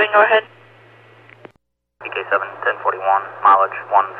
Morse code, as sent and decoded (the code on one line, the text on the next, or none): none